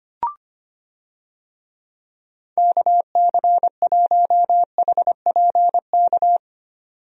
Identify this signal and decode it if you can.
Morse code, heard as a keyed tone, about 25 wpm, 700 Hz. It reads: KC1HPK
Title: USA Call Signs - 6 Characters - 25wpm